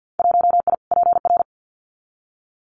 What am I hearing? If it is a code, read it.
Morse code, heard as a keyed tone, about 50 wpm, 700 Hz. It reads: 1I PR
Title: Winter Field Day Exchanges Encore 50wpm